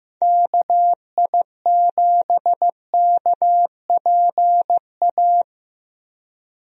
Morse code, heard as a keyed tone, about 15 wpm, 700 Hz. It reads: KI7KPA
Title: USA Call Signs - 6 Characters - 15wpm